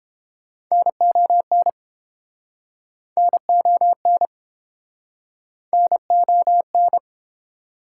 Morse code, heard as a keyed tone, about 28 wpm, 700 Hz. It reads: NON NON NON